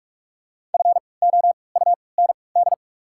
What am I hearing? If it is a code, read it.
Morse code, heard as a keyed tone, about 47 wpm, 690 Hz. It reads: F O U N D
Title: 3x Character Spacing - Top 300 Words - 45wpm